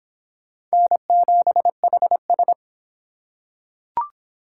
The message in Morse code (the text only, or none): N7HS